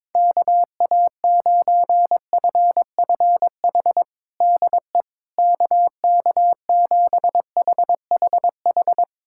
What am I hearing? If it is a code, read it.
Morse code, heard as a keyed tone, about 22 wpm, 700 Hz. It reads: KA9FFH DE KK7HHH
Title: USA Callsign DE Callsign - 6 Characters - 22wpm